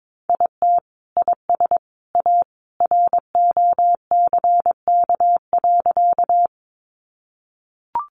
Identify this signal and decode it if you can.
Morse code, heard as a keyed tone, about 22 wpm, 700 Hz. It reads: IT IS A ROCK.
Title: Sentences from Top 400 Words 22wpm